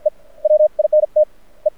Morse code, heard as a keyed tone, about 50 wpm, spaced as for 32 wpm, 610 Hz. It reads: E WINTE
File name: archived / CRSnet